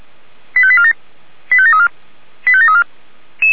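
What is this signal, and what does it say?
Five-tone selective call (ZVEI), heard as quick runs of five tones, tones of about 70 ms. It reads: 85847 85734 85734
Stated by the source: Five-tone ZVEI selective call received on SDR, feuerwehr-orlamuende.org fire brigade audio server.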